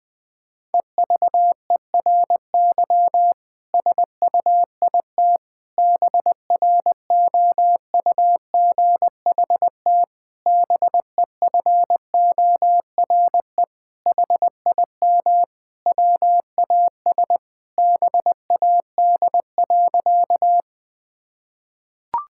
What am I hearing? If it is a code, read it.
Morse code, heard as a keyed tone, about 21 wpm, 700 Hz. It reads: EVERY SUIT BROUGHT BEFORE HIM WAS BAD.
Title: Sentences from Top 1600 Words 20wpm